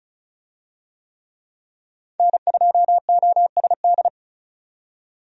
Morse code, heard as a keyed tone, about 36 wpm, 700 Hz. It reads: N2OSD
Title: USA Call Signs - 5 Characters - 35wpm